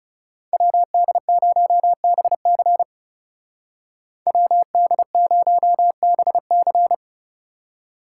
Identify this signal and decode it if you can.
Morse code, heard as a keyed tone, about 39 wpm, 700 Hz. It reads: WD0BC WD0BC